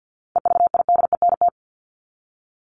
Morse code, heard as a keyed tone, about 48 wpm, 700 Hz. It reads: EVIDENT
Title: LY Suffix - Top 100 Words with Base Words 50wpm